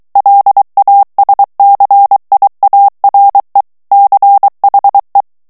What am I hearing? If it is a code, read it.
Morse code, heard as a keyed tone, about 23 wpm, 800 Hz. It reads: LASCIARE CHE
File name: morse mp3 / ITALIEN texte clair 1 vitesse 22 mots minute